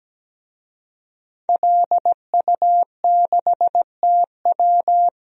Morse code, heard as a keyed tone, about 17 wpm, 700 Hz. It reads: LU6TW
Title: DXCC-100 Call Signs - 5 Characters - 17wpm